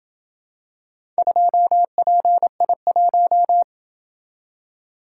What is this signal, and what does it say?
Morse code, heard as a keyed tone, about 26 wpm, 700 Hz. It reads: 2PI1